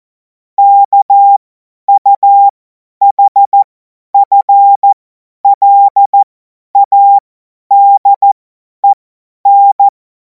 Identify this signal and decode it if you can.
Morse code, heard as a keyed tone, about 14 wpm, spaced as for 10 wpm, 800 Hz. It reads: KUHFLADEN
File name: Morsekurs / Witze / Witz 30. Juni 2026 50 F70